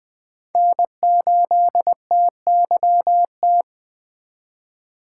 Morse code, heard as a keyed tone, about 20 wpm, 700 Hz. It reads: N8TYT
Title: USA Call Signs - 5 Characters - 2x Character Spacing - 20wpm